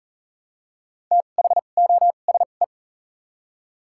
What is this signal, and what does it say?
Morse code, heard as a keyed tone, about 40 wpm, spaced as for 29 wpm, 700 Hz. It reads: THOSE
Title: 2x Character Spacing - Top 300 Words - 40wpm